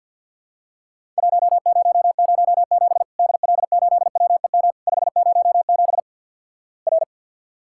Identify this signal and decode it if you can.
Morse code, heard as a keyed tone, about 54 wpm, 700 Hz. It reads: 1007 DL8WEM 507 R
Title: Worked All Europe DX Contest - QTC Exchange - 50wpm